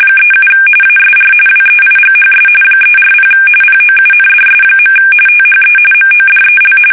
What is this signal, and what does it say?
RTTY, about 45.45 baud. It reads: 37 ASR WEIGHS 340 POUNDS. THE MODEL
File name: fsk850 / teletype-corp